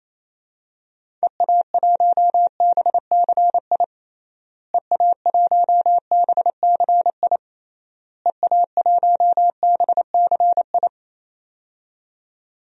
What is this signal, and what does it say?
Morse code, heard as a keyed tone, about 28 wpm, 700 Hz. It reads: EA1BCI EA1BCI EA1BCI
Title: DXCC-100 Call Signs - 6 Characters - Repeated 3x - 28wpm